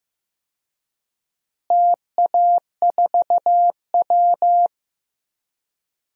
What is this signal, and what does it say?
Morse code, heard as a keyed tone, about 15 wpm, 700 Hz. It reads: TA4W
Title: DXCC-100 Call Signs - 4 Characters - Repeated 3x - 15wpm